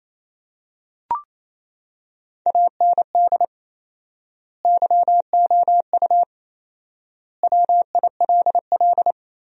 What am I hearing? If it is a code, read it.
Morse code, heard as a keyed tone, about 28 wpm, 700 Hz. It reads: AND YOU WILL
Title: 4x Word-Spacing - Sentences from Top 400 Words - 28wpm